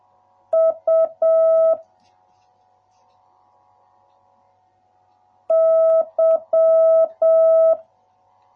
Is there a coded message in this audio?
Yes: Morse code from 0.5 to 7.7 s, heard as a keyed tone, about 7 wpm, 650 Hz. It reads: U Y